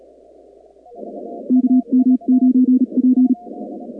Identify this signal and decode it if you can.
Morse code, heard as a keyed tone, about 36 wpm, 250 Hz. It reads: KM9P
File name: contests / 2019-cw-wpx-kt5j-k5pi- -2330 0-45